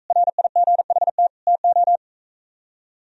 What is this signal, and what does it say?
Morse code, heard as a keyed tone, about 43 wpm, 700 Hz. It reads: RIGHT TO